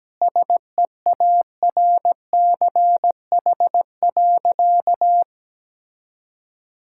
Morse code, heard as a keyed tone, about 17 wpm, 700 Hz. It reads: SEARCH.
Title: Sentences from Top 1000 Words - Review Part 5 - 17wpm